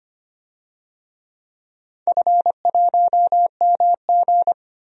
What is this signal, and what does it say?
Morse code, heard as a keyed tone, about 25 wpm, 700 Hz. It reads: F1MG
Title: DXCC-100 Call Signs - 4 Characters - Repeated 3x - 25wpm